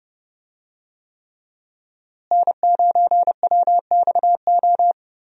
Morse code, heard as a keyed tone, about 30 wpm, 700 Hz. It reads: N9WXO
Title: USA Call Signs - 5 Characters - Repeated 3x - 30wpm